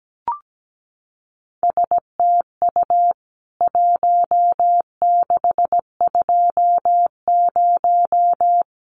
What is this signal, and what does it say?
Morse code, heard as a keyed tone, about 17 wpm, 700 Hz. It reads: STU 1620
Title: CWT Exchange 17wpm